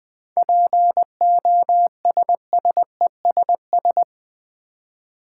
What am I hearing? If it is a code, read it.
Morse code, heard as a keyed tone, about 21 wpm, 700 Hz. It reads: POSSESS
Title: Seven Letter Words - 20wpm